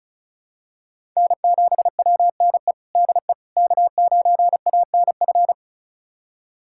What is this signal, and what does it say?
Morse code, heard as a keyed tone, about 36 wpm, 700 Hz. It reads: N7WNE DE K9ANF
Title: USA Callsign DE Callsign - 5 Characters - 35wpm